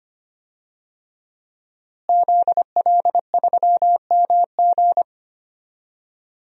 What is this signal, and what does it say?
Morse code, heard as a keyed tone, about 25 wpm, 700 Hz. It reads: ZL3MG
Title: DXCC-100 Call Signs - 5 Characters - Repeated 2x - 25wpm